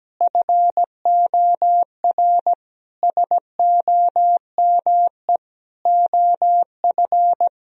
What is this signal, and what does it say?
Morse code, heard as a keyed tone, about 17 wpm, 700 Hz. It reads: FOR SOME OF
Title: Sentences from Top 1000 Words - Review Part 3 - 17wpm